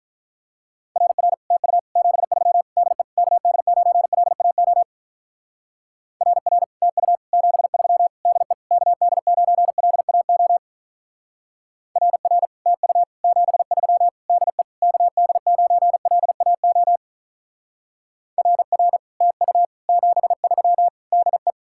Morse code, heard as a keyed tone, about 42 wpm, 700 Hz. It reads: RR TU 73 DE KD9LAO RR TU 73 DE KD9LAO RR TU 73 DE KD9LAO RR TU 73 DE
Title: Speed-Racing Common QSO Phrases - 35wpm